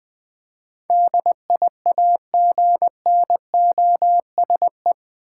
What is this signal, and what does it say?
Morse code, heard as a keyed tone, about 21 wpm, 700 Hz. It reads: DIAGNOSE
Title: Eight Letter Words - 20wpm